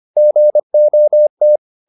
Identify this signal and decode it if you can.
Morse code, heard as a keyed tone, about 25 wpm, 600 Hz. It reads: GOT
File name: the old man and the sea part 1 25wpm